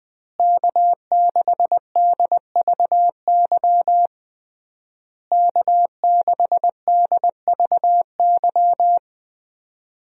Morse code, heard as a keyed tone, about 20 wpm, 700 Hz. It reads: K6DVY K6DVY
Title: USA Call Signs - 5 Characters - Repeated 3x - 20wpm